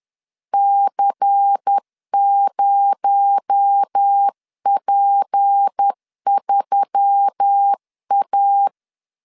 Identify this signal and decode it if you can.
Morse code, heard as a keyed tone, about 10 wpm, 800 Hz. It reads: C0P3A
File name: CW / DET / 1 6 1500C S C 10wpm 4252